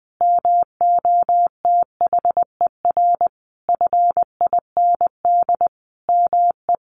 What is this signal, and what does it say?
Morse code, heard as a keyed tone, about 20 wpm, 700 Hz. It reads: MOTHER FIND ME
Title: Sentences from Top 1400 Words 20wpm